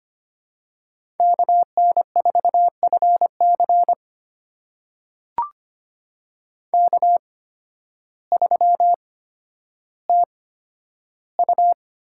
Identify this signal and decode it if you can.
Morse code, heard as a keyed tone, about 25 wpm, spaced as for 12 wpm, 700 Hz. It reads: KN4FC K3TU
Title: USA Call Signs - 8x Character Spacing - 25wpm